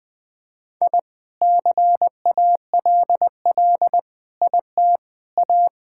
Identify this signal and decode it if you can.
Morse code, heard as a keyed tone, about 21 wpm, 700 Hz. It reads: I CALL IT A
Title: Sentences from Top 1000 Words - Review Part 6 - 20wpm